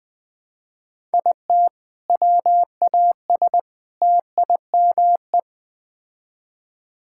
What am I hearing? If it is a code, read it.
Morse code, heard as a keyed tone, about 20 wpm, 690 Hz. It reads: IT WAS TIME
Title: Build A Phrase - 4-Word Phrases - 20wpm